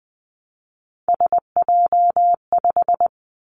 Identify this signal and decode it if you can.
Morse code, heard as a keyed tone, about 20 wpm, 700 Hz. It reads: SJ5
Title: Three Challenging Letter-Number Combinations 20wpm